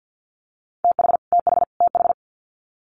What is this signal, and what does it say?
Morse code, heard as a keyed tone, about 50 wpm, 700 Hz. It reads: TH TH TH
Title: Multi-letter Phonemes Repeated 3x 50wpm